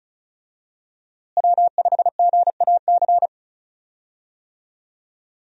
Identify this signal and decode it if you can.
Morse code, heard as a keyed tone, about 36 wpm, 700 Hz. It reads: W5GAC
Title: USA Call Signs - 5 Characters - Repeated 2x - 35wpm